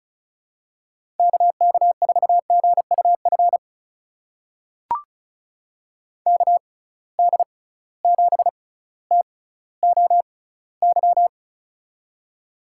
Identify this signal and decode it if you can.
Morse code, heard as a keyed tone, about 36 wpm, spaced as for 18 wpm, 700 Hz. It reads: KK4GUF KD7TOY
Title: USA Call Signs - 6 Characters - 6x Character Spacing - 35wpm